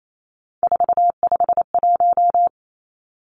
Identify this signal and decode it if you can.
Morse code, heard as a keyed tone, about 28 wpm, 700 Hz. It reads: 451
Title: Three Challenging Letter-Number Combinations 28wpm